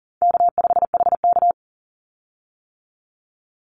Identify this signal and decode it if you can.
Morse code, heard as a keyed tone, about 40 wpm, 700 Hz. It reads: K5HK
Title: Easy Call Signs - 40wpm